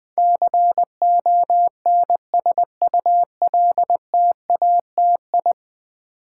Morse code, heard as a keyed tone, about 20 wpm, 700 Hz. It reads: CONSULTATI
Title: Build A Word - 5 Letters At A Time - 12-Letter Words - 20wpm